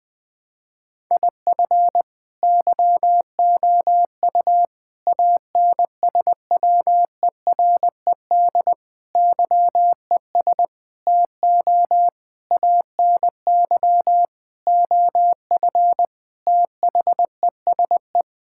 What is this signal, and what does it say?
Morse code, heard as a keyed tone, about 20 wpm, 700 Hz. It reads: IF YOU ANSWERED YES TO ANY OF THESE
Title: Sets of 9 Words - 20wpm